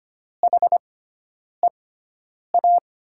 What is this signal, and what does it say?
Morse code, heard as a keyed tone, about 25 wpm, spaced as for 9 wpm, 700 Hz. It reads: HEA
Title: USA Call Signs - 6 Characters - 6x Character Spacing - 25wpm